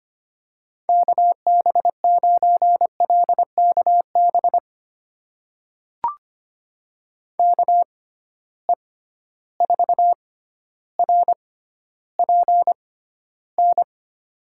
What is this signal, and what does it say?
Morse code, heard as a keyed tone, about 25 wpm, spaced as for 13 wpm, 700 Hz. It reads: KB9LKB KE4RPN
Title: USA Call Signs - 6 Characters - 6x Character Spacing - 25wpm